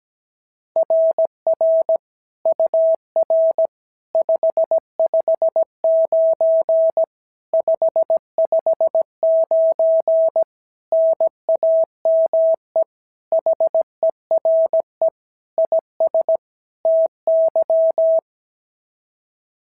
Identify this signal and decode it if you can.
Morse code, heard as a keyed tone, about 16 wpm, 650 Hz. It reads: RR UR 559 559 NAME HERE IS TY
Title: SOTA QSO Practice 17wpm